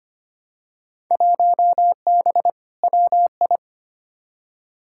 Morse code, heard as a keyed tone, about 25 wpm, 700 Hz. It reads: 1B WI
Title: Field Day Exchanges Encore - 25wpm